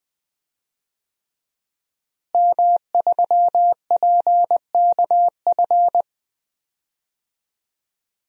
Morse code, heard as a keyed tone, about 20 wpm, 700 Hz. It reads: M3PKF
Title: DXCC-100 Call Signs - 5 Characters - Repeated 3x - 20wpm